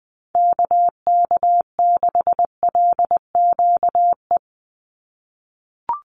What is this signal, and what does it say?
Morse code, heard as a keyed tone, about 19 wpm, 700 Hz. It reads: KK6LQE
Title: USA Call Signs 20wpm